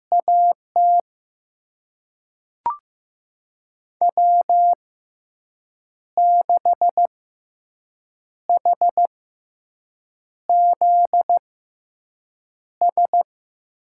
Morse code, heard as a keyed tone, about 15 wpm, spaced as for 6 wpm, 700 Hz. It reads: AT W6HZS
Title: USA Call Signs - 6x Character Spacing - 15wpm